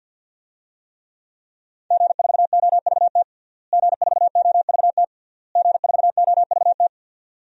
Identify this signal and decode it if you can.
Morse code, heard as a keyed tone, about 52 wpm, 700 Hz. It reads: G4OVT G4OVT G4OVT
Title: DXCC-100 Call Signs - 5 Characters - Repeated 3x - 50wpm